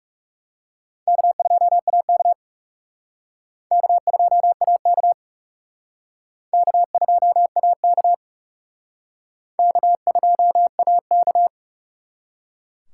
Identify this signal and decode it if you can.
Morse code, heard as a keyed tone, about 35 wpm, 700 Hz. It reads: K2AK K2AK K2AK K2AK
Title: Speed Racing - USA 1x2 Call Signs 30wpm